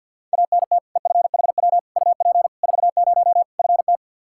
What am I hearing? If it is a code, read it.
Morse code, heard as a keyed tone, about 51 wpm, 700 Hz. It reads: ANT EFHW UP 40 FT
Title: Practice QSO - Exchange 2 - Info - 50wpm